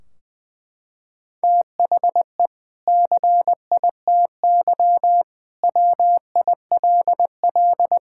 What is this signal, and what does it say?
Morse code, heard as a keyed tone, about 21 wpm, 700 Hz. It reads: THE CITY WILL
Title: Sentences from Top 1600 Words 20wpm